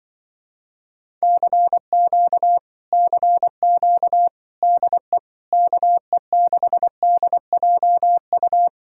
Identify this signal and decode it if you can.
Morse code, heard as a keyed tone, about 25 wpm, 700 Hz. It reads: CQ CQ DE KE6DJU
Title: Speed-Racing Common QSO Phrases Encore - 20wpm